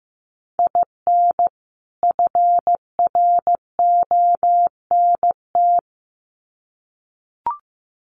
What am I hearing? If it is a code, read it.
Morse code, heard as a keyed tone, about 15 wpm, 700 Hz. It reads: IN FRONT
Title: Sets of 2 Words 15wpm